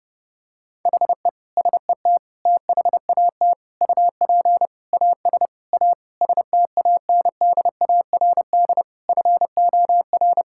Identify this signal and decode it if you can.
Morse code, heard as a keyed tone, about 30 wpm, 700 Hz. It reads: HE SET THAT UP AS A STANDARD FOR
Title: Sentences from Top 1000 Words - Review Part 6 - 30wpm